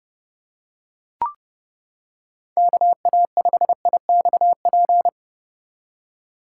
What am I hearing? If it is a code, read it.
Morse code, heard as a keyed tone, about 31 wpm, 700 Hz. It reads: KA5IXP